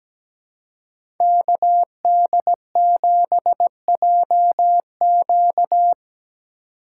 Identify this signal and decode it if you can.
Morse code, heard as a keyed tone, about 17 wpm, 700 Hz. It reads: KD7JQ
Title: USA Call Signs - 5 Characters - 4x Character Spacing - 17wpm